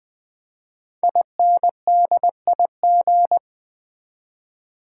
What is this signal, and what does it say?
Morse code, heard as a keyed tone, about 20 wpm, 700 Hz. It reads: INDIG
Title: Build A Word - 5 Letters At A Time - 10-Letter Words - 20wpm